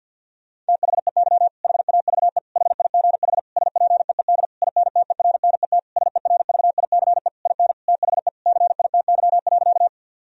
Morse code, heard as a keyed tone, about 50 wpm, 700 Hz. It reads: THEY HAVE HIGH SPEED INTERNET SERVICE IN THE CITY.
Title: Sentences from Top 1000 Words - Review Part 4 - 50wpm